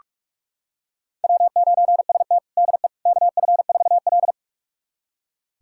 Morse code, heard as a keyed tone, about 47 wpm, 700 Hz. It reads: W9ST DE KF4L
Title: USA Callsign DE Callsign - 4 Characters - 45wpm